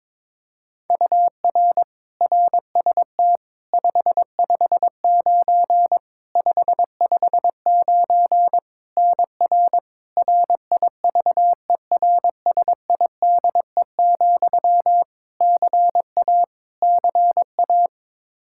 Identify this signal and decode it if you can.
Morse code, heard as a keyed tone, about 22 wpm, 700 Hz. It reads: UR RST 559 559 NR RIVERSIDE, CA CA
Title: Practice QSO - Exchange 1 - Beginning - 22wpm